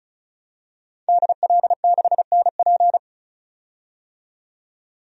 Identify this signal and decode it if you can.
Morse code, heard as a keyed tone, about 36 wpm, 700 Hz. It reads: DL6NP
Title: DXCC-100 Call Signs - 5 Characters - 35wpm